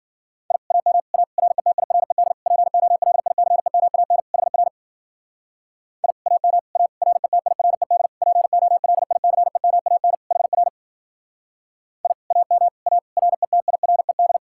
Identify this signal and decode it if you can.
Morse code, heard as a keyed tone, about 52 wpm, 700 Hz. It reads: I AM A RETIRED POLICEMAN HR I AM A RETIRED POLICEMAN HR I AM A RETIRED